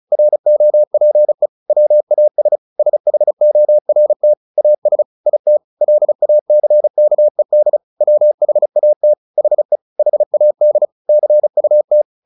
Morse code, heard as a keyed tone, about 34 wpm, 600 Hz. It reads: ROPE WAS SHORT AS IT LACKED WHAT HE HAD CUT
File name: the old man and the sea 2 35wpm